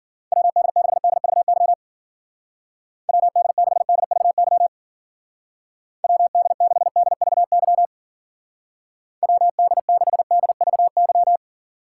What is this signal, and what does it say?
Morse code, heard as a keyed tone, about 50 wpm, 700 Hz. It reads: WD6DVY WD6DVY WD6DVY WD6DVY
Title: Speed Racing - USA 2x3 Call Signs 40wpm